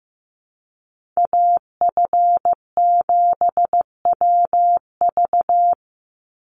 Morse code, heard as a keyed tone, about 15 wpm, 700 Hz. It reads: AF7WV
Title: USA Call Signs 15wpm